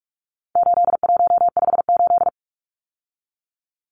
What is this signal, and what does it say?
Morse code, heard as a keyed tone, about 44 wpm, 700 Hz. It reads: 8158
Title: Numbers - 4-Digit 45wpm